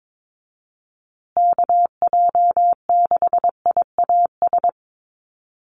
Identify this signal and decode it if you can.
Morse code, heard as a keyed tone, about 22 wpm, 700 Hz. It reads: KJ6IAS
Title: USA Call Signs Repeated 2x 22wpm